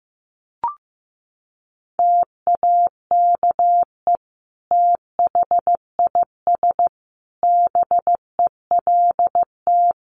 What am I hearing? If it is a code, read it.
Morse code, heard as a keyed tone, about 15 wpm, 700 Hz. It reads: TAKE THIS BELT
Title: Sentences from Top 1400 Words 15wpm